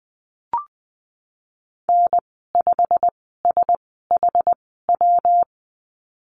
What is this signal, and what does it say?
Morse code, heard as a keyed tone, about 20 wpm, spaced as for 15 wpm, 700 Hz. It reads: N5SHW
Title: 122 Lesson 20g - Number 3 - Call Signs 20wpm